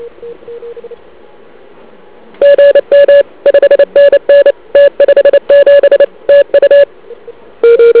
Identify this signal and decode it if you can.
Morse code, heard as a keyed tone, about 29 wpm, 1700 Hz, with noise. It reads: GM 5NN T57 TU
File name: ARCHIV / KX3 / CW pileup